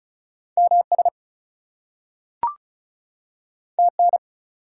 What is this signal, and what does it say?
Morse code, heard as a keyed tone, about 36 wpm, 700 Hz. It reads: MS TN
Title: US States and Canadian Provinces 35wpm